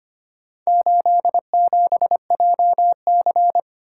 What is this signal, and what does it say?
Morse code, heard as a keyed tone, about 25 wpm, 700 Hz. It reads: 87JC
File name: practice / 25wpm / Four Letter-Number Combinations - 25wpm